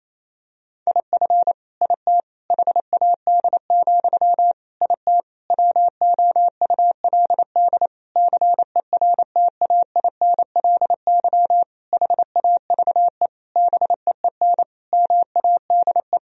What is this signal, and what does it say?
Morse code, heard as a keyed tone, about 28 wpm, 700 Hz. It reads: IF IT HAD, IT WOULD CERTAINLY HAVE BEEN MADE
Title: Sentences from Top 1000 Words - Review Part 3 - 28wpm